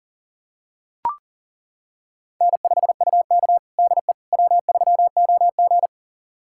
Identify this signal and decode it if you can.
Morse code, heard as a keyed tone, about 40 wpm, 700 Hz. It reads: N5UK DE W3OG
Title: USA Callsign DE Callsign - 4 Characters - 40wpm